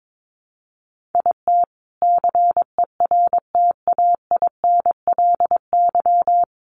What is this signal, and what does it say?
Morse code, heard as a keyed tone, about 22 wpm, 700 Hz. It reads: IT CERTAINLY
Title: Sentences from Top 1000 Words 22wpm